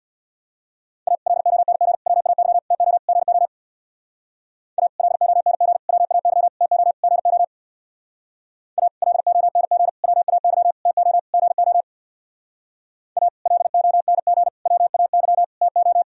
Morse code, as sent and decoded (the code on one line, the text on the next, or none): .- / .-.. --- -. --. / .-- .- -.-- / - --- / --. --- / .- / .-.. --- -. --. / .-- .- -.-- / - --- / --. --- / .- / .-.. --- -. --. / .-- .- -.-- / - --- / --. --- / .- / .-.. --- -. --. / .-- .- -.-- / - ---
A LONG WAY TO GO A LONG WAY TO GO A LONG WAY TO GO A LONG WAY TO